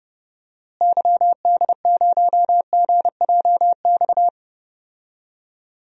Morse code, heard as a keyed tone, about 30 wpm, 700 Hz. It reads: YD0GJX
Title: DXCC-100 Call Signs - 6 Characters - Repeated 2x - 30wpm